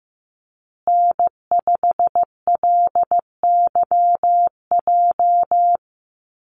Morse code, heard as a keyed tone, about 15 wpm, 700 Hz. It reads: N5LYJ
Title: Call Signs Repeated 2x - 15wpm